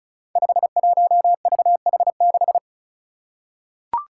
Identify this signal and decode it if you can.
Morse code, heard as a keyed tone, about 36 wpm, 700 Hz. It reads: 51VH6
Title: Five Challenging Letter-Number Combinations 35wpm